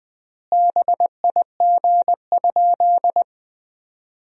Morse code, heard as a keyed tone, about 20 wpm, 700 Hz. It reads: BIG?
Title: 4x Word-Spacing - Sentences from Top 200 Words - 20wpm